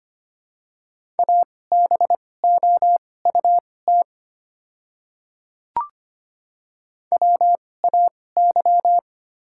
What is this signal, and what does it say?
Morse code, heard as a keyed tone, about 25 wpm, spaced as for 18 wpm, 700 Hz. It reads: ABOUT WAY
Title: 2x Character Spacing - Top 500 Words - 25wpm